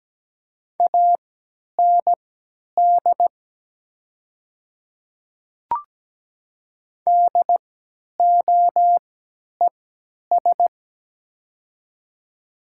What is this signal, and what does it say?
Morse code, heard as a keyed tone, about 17 wpm, spaced as for 10 wpm, 700 Hz. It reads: AND DOES
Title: 3x Character Spacing - Top 200 Words - 17wpm